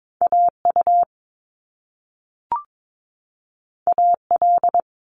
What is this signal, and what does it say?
Morse code, heard as a keyed tone, about 22 wpm, 700 Hz. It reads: AU AL
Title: Top Two Letter Combinations in Words 22wpm